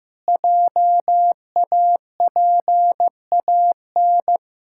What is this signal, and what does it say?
Morse code, heard as a keyed tone, about 15 wpm, 700 Hz. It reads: JAPAN
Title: Pangram Challenge v2 - 15wpm